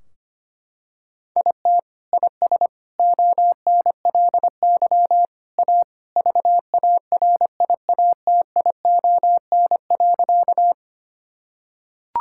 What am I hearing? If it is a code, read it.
Morse code, heard as a keyed tone, about 25 wpm, 700 Hz. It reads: IT IS ONLY A VARIATION.
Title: Sentences from Top 1600 Words 25wpm